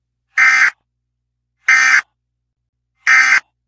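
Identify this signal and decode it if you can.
Emergency Alert System (SAME) tones, heard as harsh buzzing bursts. It is heard from 0.3 to 3.4 s.
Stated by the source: Alert tones heard on ZT dispatch via the 7500 repeater, received on a scanner.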